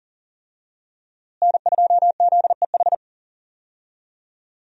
Morse code, heard as a keyed tone, about 40 wpm, 700 Hz. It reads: N2ZEH
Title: USA Call Signs - 5 Characters - Repeated 2x - 40wpm